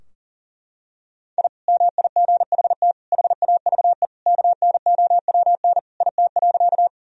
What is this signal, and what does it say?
Morse code, heard as a keyed tone, about 40 wpm, 700 Hz. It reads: I MIGHT HAVE KNOWN IT.
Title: Sentences from Top 1000 Words - Review Part 3 - 40wpm